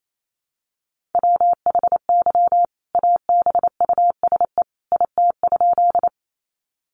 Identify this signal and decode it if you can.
Morse code, heard as a keyed tone, about 28 wpm, 700 Hz. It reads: WHY ABUSE IT?
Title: Sentences from Top 1300 Words 28wpm